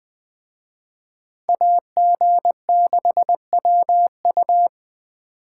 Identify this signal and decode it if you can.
Morse code, heard as a keyed tone, about 22 wpm, 700 Hz. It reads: AG6WU